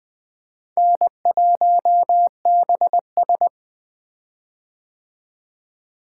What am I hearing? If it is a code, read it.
Morse code, heard as a keyed tone, about 20 wpm, 700 Hz. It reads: N1BS
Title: 2023 POTA Activator Call Signs - 20wpm